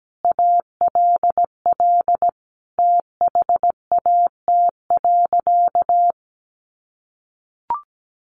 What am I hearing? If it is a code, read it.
Morse code, heard as a keyed tone, about 17 wpm, 700 Hz. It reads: ALL THAT.
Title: Sentences from Top 1200 Words 17wpm